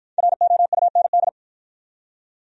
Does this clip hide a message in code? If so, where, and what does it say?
Yes: Morse code from 0.2 to 1.3 s, heard as a keyed tone, about 53 wpm, 700 Hz. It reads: ROUND